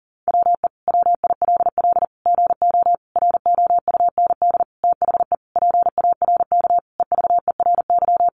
Morse code, heard as a keyed tone, about 40 wpm, 700 Hz. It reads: WE WILL GO ROUND THE PARK EVERY
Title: Sentences from Top 1300 Words 40wpm